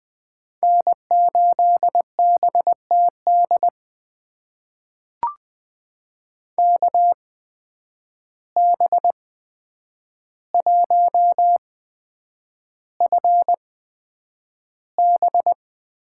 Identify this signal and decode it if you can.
Morse code, heard as a keyed tone, about 20 wpm, spaced as for 9 wpm, 700 Hz. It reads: N8BTD KB1FB
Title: USA Call Signs - 5 Characters - 8x Character Spacing - 20wpm